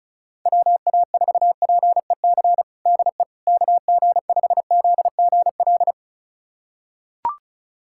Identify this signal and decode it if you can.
Morse code, heard as a keyed tone, about 34 wpm, 700 Hz. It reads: WA4PEC DE KG5ZGL
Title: USA Callsign DE Callsign - 6 Characters - 35wpm